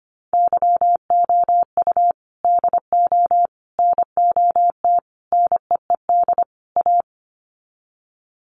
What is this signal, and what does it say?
Morse code, heard as a keyed tone, about 25 wpm, 700 Hz. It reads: YOU DO NOT NEED A
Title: Sets of 5 Words Encore 25wpm